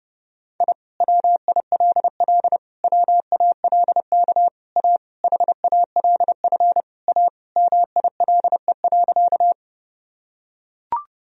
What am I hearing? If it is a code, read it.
Morse code, heard as a keyed tone, about 30 wpm, 700 Hz. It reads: I WILL WALK A HALF A MILE.
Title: Sentences from Top 1000 Words - Review Part 1 - 30wpm